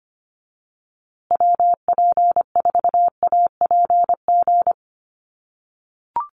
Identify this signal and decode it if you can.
Morse code, heard as a keyed tone, about 25 wpm, 700 Hz. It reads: WP4APG
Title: USA Call Signs Repeated 3x 25wpm